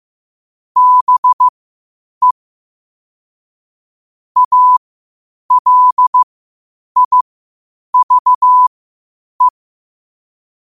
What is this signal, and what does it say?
Morse code, heard as a keyed tone, about 15 wpm, spaced as for 8 wpm, 1000 Hz. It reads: BE ALIVE
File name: qotdfiles / 8wpm